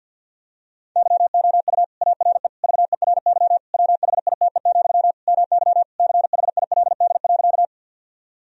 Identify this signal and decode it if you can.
Morse code, heard as a keyed tone, about 50 wpm, 700 Hz. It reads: YOU ARE VERY WHITE, MY CHILD.
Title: Sentences from Top 1000 Words - Review Part 1 - 50wpm